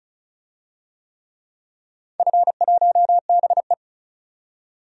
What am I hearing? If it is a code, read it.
Morse code, heard as a keyed tone, about 34 wpm, 700 Hz. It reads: F1BE
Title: DXCC-100 Call Signs - 4 Characters - 35wpm